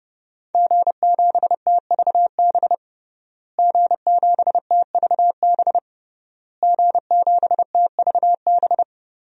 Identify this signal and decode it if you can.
Morse code, heard as a keyed tone, about 30 wpm, 700 Hz. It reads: G7TVB G7TVB G7TVB
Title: DXCC-100 Call Signs - 5 Characters - Repeated 3x - 30wpm